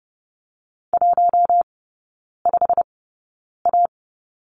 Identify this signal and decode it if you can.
Morse code, heard as a keyed tone, about 30 wpm, spaced as for 9 wpm, 700 Hz. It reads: 15A
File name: Lesson 17e - Letter u - Three Characters 30wpm